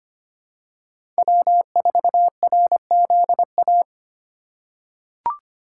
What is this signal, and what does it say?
Morse code, heard as a keyed tone, about 25 wpm, 700 Hz. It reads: W4RZA
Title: USA Call Signs - 5 Characters - 4x Character Spacing - 25wpm